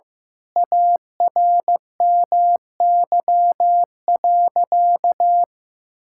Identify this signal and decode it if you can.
Morse code, heard as a keyed tone, about 15 wpm, 700 Hz. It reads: ARMY.
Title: Sentences from Top 1000 Words - Review Part 6 - 15wpm